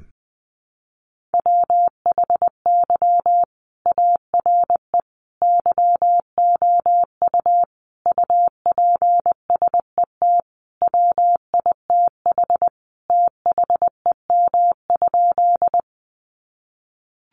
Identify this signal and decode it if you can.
Morse code, heard as a keyed tone, about 20 wpm, 700 Hz. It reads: WHY ARE YOU UPSET WITH THEM?
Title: Sentences from Top 800 Words 20wpm